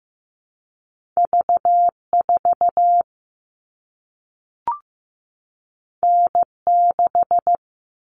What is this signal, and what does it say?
Morse code, heard as a keyed tone, about 15 wpm, 700 Hz. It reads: V4 N6
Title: All Cut Numbers - 2-Digit 15wpm